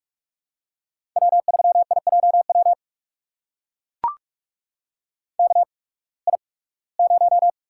Morse code, heard as a keyed tone, about 44 wpm, spaced as for 23 wpm, 700 Hz. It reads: W3IJW KI0
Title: USA Call Signs - 5 Characters - 8x Character Spacing - 45wpm